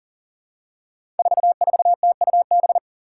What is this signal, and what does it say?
Morse code, heard as a keyed tone, about 40 wpm, 700 Hz. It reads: 44TUB
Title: Five Challenging Letter-Number Combinations 40wpm